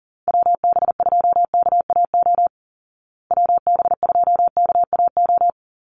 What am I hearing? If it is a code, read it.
Morse code, heard as a keyed tone, about 37 wpm, 700 Hz. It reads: WB2KAO WB2KAO
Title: Call Signs Repeated 3x - 40wpm